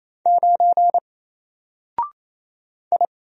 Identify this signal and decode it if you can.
Morse code, heard as a keyed tone, about 27 wpm, 700 Hz. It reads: 9 I
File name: Single-Letter-Number-28wpm